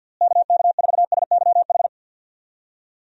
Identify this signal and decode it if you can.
Morse code, heard as a keyed tone, about 48 wpm, 700 Hz. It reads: KK4SYH
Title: USA Call Signs - 6 Characters - Repeated 3x - 50wpm